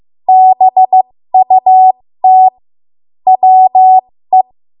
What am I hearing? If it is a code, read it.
Morse code, heard as a keyed tone, about 15 wpm, spaced as for 13 wpm, 750 Hz. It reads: BUT WE
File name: Archive / 13 WPM / 13WPM